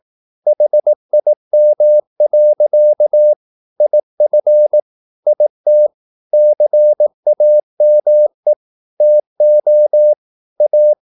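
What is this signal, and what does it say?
Morse code, heard as a keyed tone, about 18 wpm, 600 Hz. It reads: HIM. IF IT CAME TO A
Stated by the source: Morse practice at 18 wpm, English text, from CW med Gnister.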